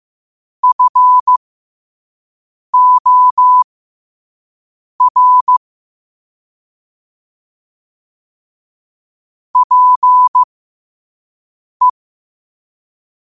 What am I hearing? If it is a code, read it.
Morse code, heard as a keyed tone, about 15 wpm, spaced as for 5 wpm, 1000 Hz. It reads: FOR PE
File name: qotdfiles / 5wpm